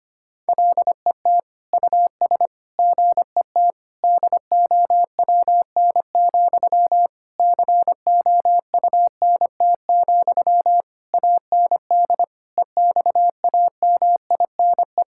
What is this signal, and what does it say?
Morse code, heard as a keyed tone, about 25 wpm, 700 Hz. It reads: LET US GET DOWN, COUNT, AND EXAMINE
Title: Sentences from Top 1600 Words 25wpm